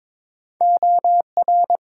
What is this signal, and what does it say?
Morse code, heard as a keyed tone, about 22 wpm, 700 Hz. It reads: OR
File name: practice / 22wpm / Build A Phrase - 2-Word Phrases - 22wpm